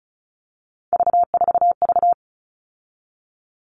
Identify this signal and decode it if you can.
Morse code, heard as a keyed tone, about 34 wpm, 700 Hz. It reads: V4V
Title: All Cut Numbers - 3-Digit 35wpm